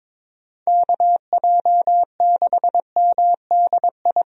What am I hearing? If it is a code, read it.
Morse code, heard as a keyed tone, about 22 wpm, 700 Hz. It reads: KJ6MDI
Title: USA Call Signs - 6 Characters - 22wpm